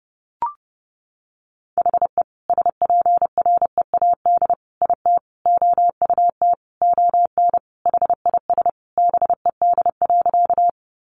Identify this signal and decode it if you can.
Morse code, heard as a keyed tone, about 30 wpm, 700 Hz. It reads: HE SPREAD IT OUT ON HIS BED.
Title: Sentences from Top 1200 Words 30wpm